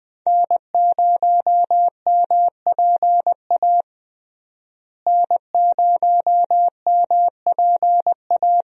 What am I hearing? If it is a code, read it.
Morse code, heard as a keyed tone, about 20 wpm, 700 Hz. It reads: N0MPA N0MPA
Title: USA Call Signs - 5 Characters - Repeated 2x - 20wpm